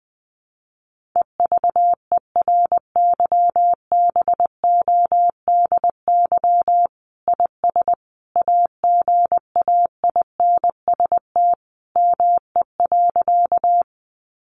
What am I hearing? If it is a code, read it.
Morse code, heard as a keyed tone, about 20 wpm, 700 Hz. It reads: EVERYBODY IS AGAINST ME.
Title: Sentences from Top 1400 Words 20wpm